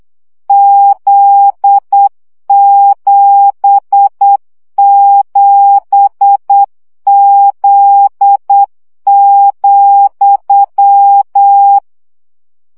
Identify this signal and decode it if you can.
Morse code, heard as a keyed tone, about 8 wpm, 800 Hz. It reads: Z77Z,
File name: CW gyakorlas z 7 vesszo 7 baud